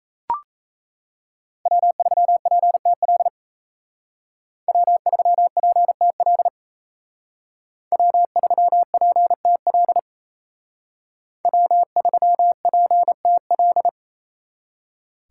Morse code, heard as a keyed tone, about 36 wpm, 700 Hz. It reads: W3PTL W3PTL W3PTL W3PTL